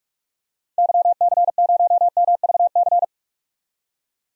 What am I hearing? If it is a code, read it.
Morse code, heard as a keyed tone, about 46 wpm, 700 Hz. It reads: YC0MVC
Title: DXCC-100 Call Signs - 6 Characters - 45wpm